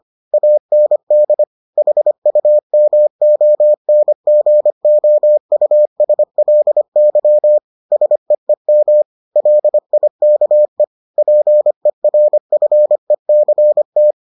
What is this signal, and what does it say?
Morse code, heard as a keyed tone, about 25 wpm, 600 Hz. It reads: AND HUMONGOUSLY SEEM LIKE PERFECT